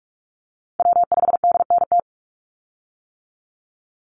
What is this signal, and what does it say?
Morse code, heard as a keyed tone, about 46 wpm, 700 Hz. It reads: W5DNT
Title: Call Signs - 45wpm